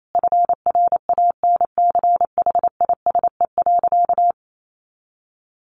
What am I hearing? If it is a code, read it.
Morse code, heard as a keyed tone, about 28 wpm, 700 Hz. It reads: FRANCHISE.
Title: A Sound Of Thunder 28wpm